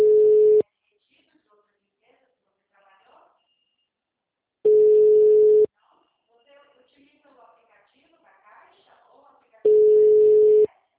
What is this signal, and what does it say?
Ringback tone (Europe, 425 Hz), with speech.
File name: out- -2031- - - .35805